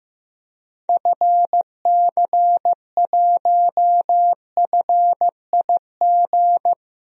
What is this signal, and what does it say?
Morse code, heard as a keyed tone, about 15 wpm, 700 Hz. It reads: FC1FIG
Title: DXCC-100 Call Signs - 6 Characters - 15wpm